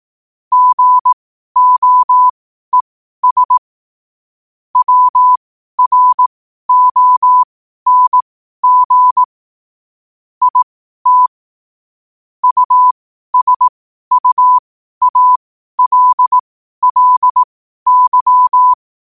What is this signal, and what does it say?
Morse code, heard as a keyed tone, about 18 wpm, spaced as for 12 wpm, 1000 Hz. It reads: GOES WRONG IT USUALLY
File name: qotdfiles / 12wpm